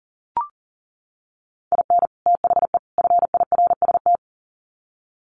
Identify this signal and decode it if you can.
Morse code, heard as a keyed tone, about 40 wpm, 700 Hz. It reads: IN THE FIRST
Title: Sets of 3 Words 40wpm